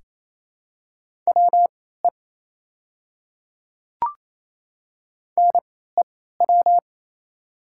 Morse code, heard as a keyed tone, about 27 wpm, spaced as for 16 wpm, 690 Hz. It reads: WE NEW
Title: 3x Character Spacing - Top 200 Words - 28wpm